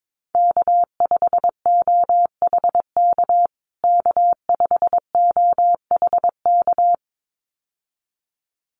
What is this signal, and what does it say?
Morse code, heard as a keyed tone, about 22 wpm, 700 Hz. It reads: K5OHK K5OHK
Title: USA Call Signs Repeated 2x 22wpm